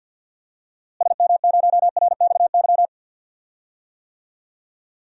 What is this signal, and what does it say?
Morse code, heard as a keyed tone, about 52 wpm, 700 Hz. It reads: SM0RXY
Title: DXCC-100 Call Signs - 6 Characters - Repeated 3x - 50wpm